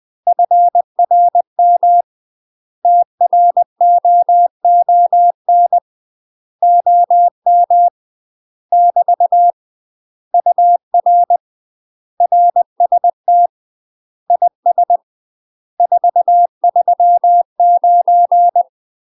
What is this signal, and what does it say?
Morse code, heard as a keyed tone, about 20 wpm, 700 Hz. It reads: FRM TROON OM <BT> UR RST IS 439
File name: UK MASTER QSOs TRAINING / 20WPM / UK MASTER QSOs 0056